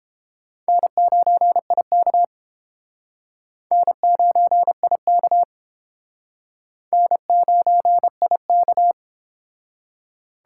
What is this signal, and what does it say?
Morse code, heard as a keyed tone, about 30 wpm, 700 Hz. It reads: N9IK N9IK N9IK